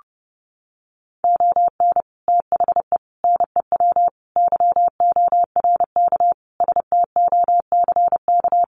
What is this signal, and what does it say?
Morse code, heard as a keyed tone, about 30 wpm, 700 Hz. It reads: ON THE NEW YORK STOCK
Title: Sets of 5 Words Encore 30wpm